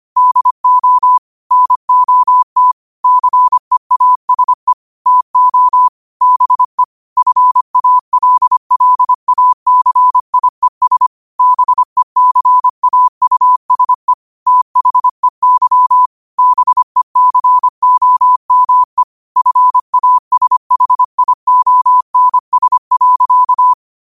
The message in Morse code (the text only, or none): DO NOT CEASE TO BE FALLACIES BECAUSE THEY BECOME FASHIONS.